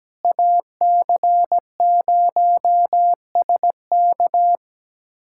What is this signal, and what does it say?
Morse code, heard as a keyed tone, about 17 wpm, 700 Hz. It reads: AC0SK
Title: USA Callsign DE Callsign - 5 Characters - 17wpm